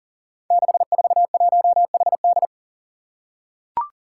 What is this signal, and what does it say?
Morse code, heard as a keyed tone, about 37 wpm, 700 Hz. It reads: 641HD
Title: Five Challenging Letter-Number Combinations 40wpm